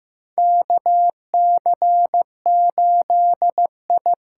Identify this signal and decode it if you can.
Morse code, heard as a keyed tone, about 15 wpm, 700 Hz. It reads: KC8I
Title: USA Call Signs - 6 Characters - 6x Character Spacing - 15wpm